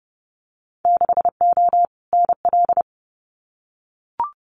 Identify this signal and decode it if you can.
Morse code, heard as a keyed tone, about 30 wpm, 700 Hz. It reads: 6O NL
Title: Winter Field Day Exchanges Encore 30wpm